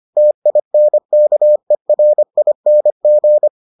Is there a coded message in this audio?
Yes: Morse code from 0.2 to 3.5 s, heard as a keyed tone, about 25 wpm, 600 Hz. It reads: TINKERING